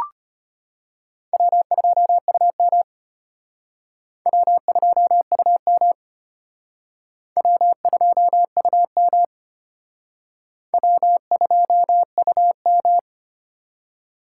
Morse code, heard as a keyed tone, about 34 wpm, 700 Hz. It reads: W2UM W2UM W2UM W2UM